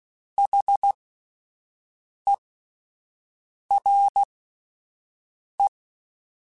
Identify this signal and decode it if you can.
Morse code, heard as a keyed tone, about 16 wpm, spaced as for 5 wpm, 800 Hz. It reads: HERE
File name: (spaced) / QSOs Spaced-mp3s / WD0BGP 1 @10 sp